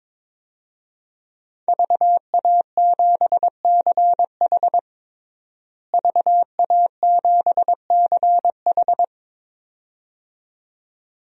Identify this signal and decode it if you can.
Morse code, heard as a keyed tone, about 22 wpm, 700 Hz. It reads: VA7CH VA7CH
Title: DXCC-100 Call Signs - 5 Characters - Repeated 2x - 22wpm